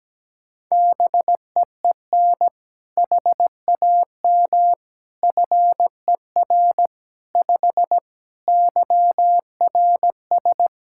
Morse code, heard as a keyed tone, about 17 wpm, 700 Hz. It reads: BEEN HAM FER 5 YRS
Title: Practice QSO - Exchange 3 - More Info - 17wpm